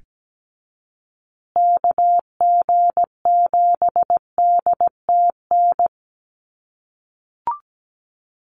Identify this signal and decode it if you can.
Morse code, heard as a keyed tone, about 17 wpm, 700 Hz. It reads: KG7DTN
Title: USA Call Signs Repeated 2x 17wpm